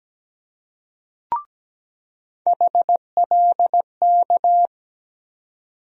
Morse code, heard as a keyed tone, about 17 wpm, 700 Hz. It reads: HLK
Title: USA Call Sign Postfixes - 3 Letters 17wpm